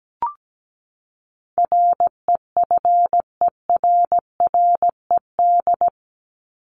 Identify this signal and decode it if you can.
Morse code, heard as a keyed tone, about 17 wpm, 700 Hz. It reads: REFERRED
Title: ED Suffix - Top 100 Words with Base Words 17wpm